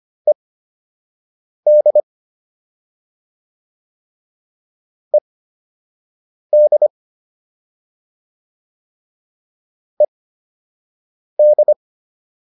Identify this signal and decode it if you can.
Morse code, heard as a keyed tone, about 25 wpm, spaced as for 6 wpm, 600 Hz. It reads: ED ED ED